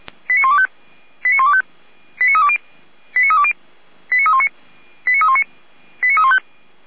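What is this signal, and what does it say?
Five-tone selective call (ZVEI), heard as quick runs of five tones, tones of about 70 ms. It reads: 89125 89125 89320 89320 89319 89319 89315
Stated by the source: ZVEI five-tone selective call received with an SDR.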